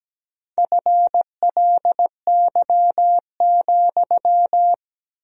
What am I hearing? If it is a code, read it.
Morse code, heard as a keyed tone, about 17 wpm, 700 Hz. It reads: FLY,
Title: 4x Word-Spacing - Sentences from Top 500 Words - 17wpm